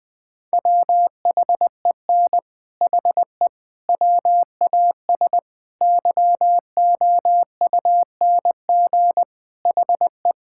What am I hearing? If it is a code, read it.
Morse code, heard as a keyed tone, about 20 wpm, 700 Hz. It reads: WHEN HE WAS YOUNG HE
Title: Sentences with 3-Word Phrases - 20wpm